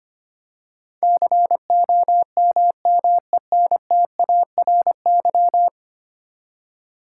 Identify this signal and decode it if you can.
Morse code, heard as a keyed tone, about 25 wpm, 700 Hz. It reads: COMMENTARY
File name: Ten Letter Words - 25wpm